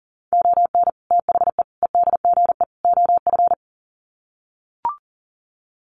Morse code, heard as a keyed tone, about 40 wpm, 700 Hz. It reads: ON THE EDGE OF
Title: Sets of 4 Words 40wpm